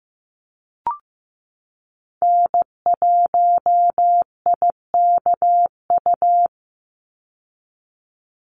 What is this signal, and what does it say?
Morse code, heard as a keyed tone, about 15 wpm, 700 Hz. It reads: N1IKU
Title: USA Call Signs 15wpm